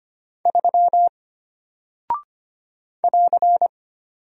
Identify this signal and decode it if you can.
Morse code, heard as a keyed tone, about 25 wpm, 700 Hz. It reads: 3 <AR>
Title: Single Character (Letter:Number:Punctuation:Prosign) Alt 25wpm